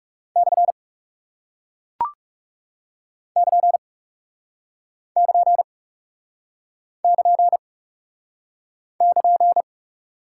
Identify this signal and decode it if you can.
Morse code, heard as a keyed tone, about 41 wpm, 700 Hz. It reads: / <KN> <KN> <KN> <KN>